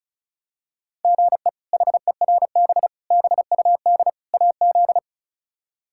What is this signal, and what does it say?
Morse code, heard as a keyed tone, about 36 wpm, 700 Hz. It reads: GE HERB BUD AZ
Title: K1USN SST QSO Practice - 35wpm